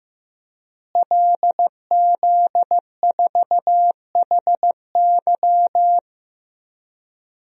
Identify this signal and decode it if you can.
Morse code, heard as a keyed tone, about 15 wpm, 700 Hz. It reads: LZ4HY
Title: DXCC-100 Call Signs - 5 Characters - Repeated 3x - 15wpm